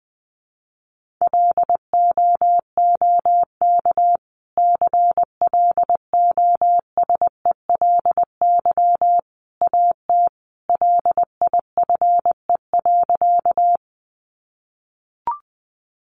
Morse code, heard as a keyed tone, about 19 wpm, 700 Hz. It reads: LOOK CLOSELY AT LIFE.
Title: Sentences from Top 1400 Words 20wpm